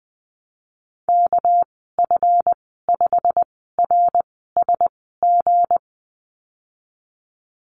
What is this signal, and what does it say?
Morse code, heard as a keyed tone, about 20 wpm, spaced as for 14 wpm, 700 Hz. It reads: KF5RSG
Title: 238 Lesson 38g - Letter q - Call Signs 20wpm